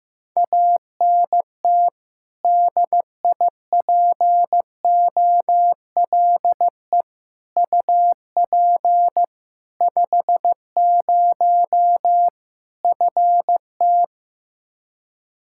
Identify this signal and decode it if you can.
Morse code, heard as a keyed tone, about 15 wpm, 700 Hz. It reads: ANT DIPOLE UP 50 FT
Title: Practice QSO - Exchange 2 - Info - 15wpm